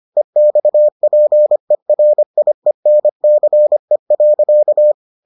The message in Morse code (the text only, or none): EXPERIENCE.